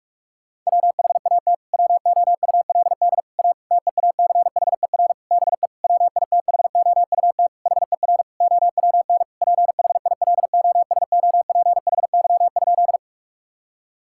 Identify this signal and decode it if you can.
Morse code, heard as a keyed tone, about 42 wpm, 700 Hz. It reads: WHAT WOULD A TEACHER BE WITHOUT HER OWN PHILOSOPHY?
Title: Sentences from Top 1000 Words - Review Part 6 - 45wpm